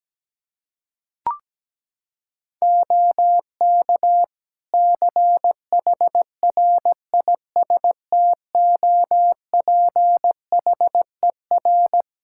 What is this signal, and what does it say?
Morse code, heard as a keyed tone, about 17 wpm, 700 Hz. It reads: OK CHRISTOPHER
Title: Practice QSO - Exchange 4 - Ending - 17wpm